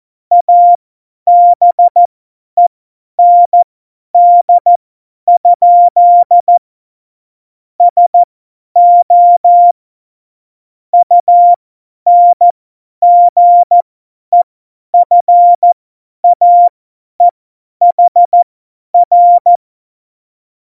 Morse code, heard as a keyed tone, about 14 wpm, spaced as for 10 wpm, 700 Hz. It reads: ABEND? SO UNGEFAEHR
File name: Morsekurs / Witze / Witz 11 August 2020, 50 F70